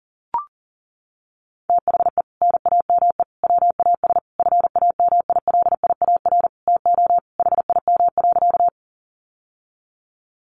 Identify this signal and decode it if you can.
Morse code, heard as a keyed tone, about 40 wpm, 700 Hz. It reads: THE NAME WAS FAMILIAR TO HIM.
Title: Sentences from Top 1000 Words 40wpm